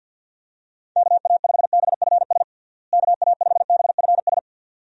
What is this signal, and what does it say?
Morse code, heard as a keyed tone, about 47 wpm, 700 Hz. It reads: KA5BFS KA5BFS
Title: USA Call Signs - 6 Characters - Repeated 2x - 50wpm